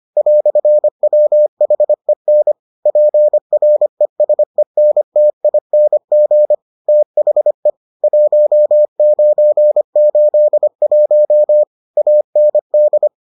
Morse code, heard as a keyed tone, about 25 wpm, 600 Hz. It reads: "WHEN PRESENTING THE 1981 AND